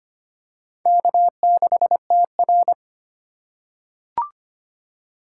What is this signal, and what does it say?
Morse code, heard as a keyed tone, about 25 wpm, 700 Hz. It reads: K6TR
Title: USA Call Signs - 4x Character Spacing - 25wpm